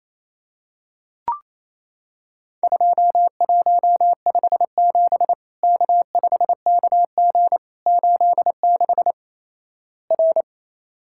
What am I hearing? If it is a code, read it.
Morse code, heard as a keyed tone, about 28 wpm, 700 Hz. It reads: 2157 K5KG 86 R
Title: Worked All Europe DX Contest - QTC Exchange - 28wpm